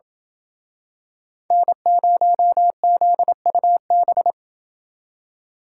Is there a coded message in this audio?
Yes: Morse code from 1.5 to 4.3 s, heard as a keyed tone, about 27 wpm, 700 Hz. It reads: N0ZUB